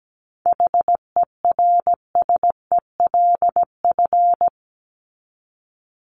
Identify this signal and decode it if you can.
Morse code, heard as a keyed tone, about 17 wpm, 700 Hz. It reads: HERSELF
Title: Top 1751-2000 Words 17wpm